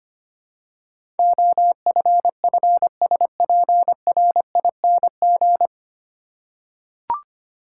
Morse code, heard as a keyed tone, about 25 wpm, 700 Hz. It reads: OFFSPRING
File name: practice / 25wpm / 9 Letter Words - First 3 Letters Given - 25wpm